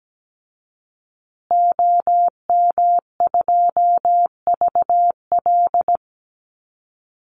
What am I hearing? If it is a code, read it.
Morse code, heard as a keyed tone, about 17 wpm, 700 Hz. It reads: OM2VL
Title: Call Signs Repeated 2x - 17wpm